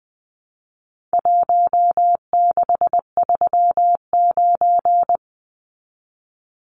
Morse code, heard as a keyed tone, about 20 wpm, 700 Hz. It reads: 1639
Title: Numbers - Contest Style - 4-Digit 20wpm